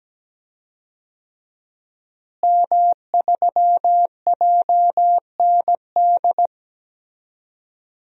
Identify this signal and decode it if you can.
Morse code, heard as a keyed tone, about 17 wpm, 700 Hz. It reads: M3JND
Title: DXCC-100 Call Signs - 5 Characters - Repeated 3x - 17wpm